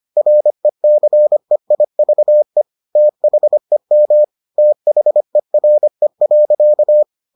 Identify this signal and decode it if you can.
Morse code, heard as a keyed tone, about 25 wpm, 600 Hz. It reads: RECEIVE THEM THERE.